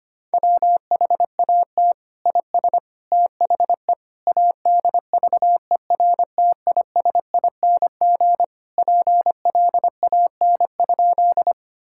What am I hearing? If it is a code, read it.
Morse code, heard as a keyed tone, about 25 wpm, 700 Hz. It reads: WHAT IS THE ADVERTISING PLAN?
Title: Sentences from Top 1000 Words - Review Part 6 - 25wpm